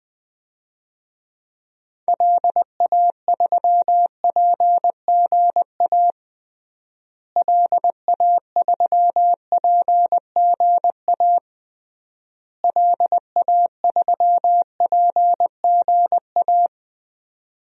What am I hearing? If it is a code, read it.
Morse code, heard as a keyed tone, about 20 wpm, 700 Hz. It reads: LA3PGA LA3PGA LA3PGA
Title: DXCC-100 Call Signs - 6 Characters - Repeated 3x - 20wpm